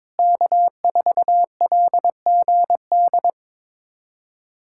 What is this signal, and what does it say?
Morse code, heard as a keyed tone, about 22 wpm, 700 Hz. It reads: K4LGD
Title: USA Callsign DE Callsign - 5 Characters - 22wpm